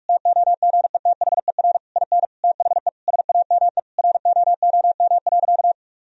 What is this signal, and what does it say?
Morse code, heard as a keyed tone, about 46 wpm, 700 Hz. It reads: TOGETHER IN THE SAME ROOM.
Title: Sentences from Top 1000 Words - Review Part 3 - 45wpm